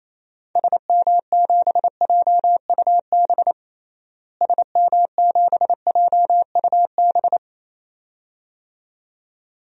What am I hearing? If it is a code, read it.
Morse code, heard as a keyed tone, about 28 wpm, 700 Hz. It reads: SM7JUB SM7JUB
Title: DXCC-100 Call Signs - 6 Characters - Repeated 2x - 28wpm